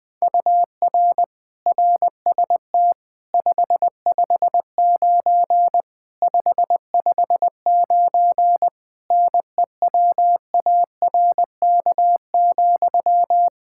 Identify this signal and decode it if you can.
Morse code, heard as a keyed tone, about 21 wpm, 700 Hz. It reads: UR RST 559 559 NEWARK,
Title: Practice QSO - Exchange 1 - Beginning - 20wpm